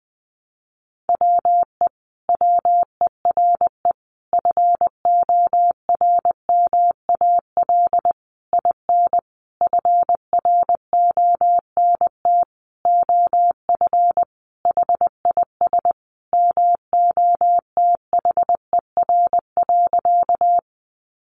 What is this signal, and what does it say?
Morse code, heard as a keyed tone, about 20 wpm, 700 Hz. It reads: WE WERE FORMAL IN FRONT OF HIS MOTHER.
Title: Sentences from Top 1300 Words 20wpm